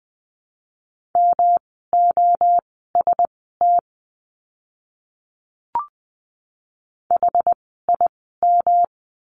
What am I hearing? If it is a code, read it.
Morse code, heard as a keyed tone, about 20 wpm, spaced as for 15 wpm, 700 Hz. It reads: MOST HIM
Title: 139 Lesson 23f - Letter f - Words 20wpm